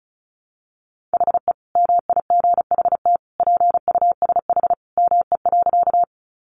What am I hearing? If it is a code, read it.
Morse code, heard as a keyed tone, about 34 wpm, 700 Hz. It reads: HE MIGHT PUSH ME.
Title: Sentences from Top 1300 Words 35wpm